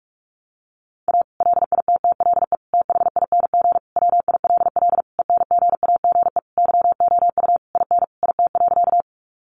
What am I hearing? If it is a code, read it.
Morse code, heard as a keyed tone, about 44 wpm, 700 Hz. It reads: A LITTLE THING WILL ENGAGE YOU IN IT.
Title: Sentences from Top 1500 Words 45wpm